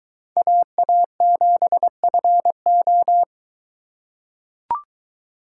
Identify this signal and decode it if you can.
Morse code, heard as a keyed tone, about 22 wpm, 700 Hz. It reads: AA7FO